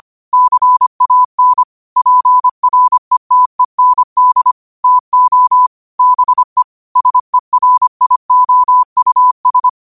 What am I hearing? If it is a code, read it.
Morse code, heard as a keyed tone, about 25 wpm, 1000 Hz. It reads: CAN PRETEND TO BE SERIOUS
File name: qotdfiles / 25wpm